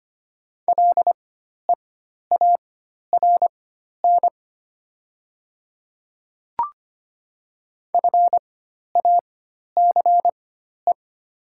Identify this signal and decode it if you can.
Morse code, heard as a keyed tone, about 25 wpm, spaced as for 12 wpm, 700 Hz. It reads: LEARN FACE
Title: 4x Character Spacing - Top 300 Words - 25wpm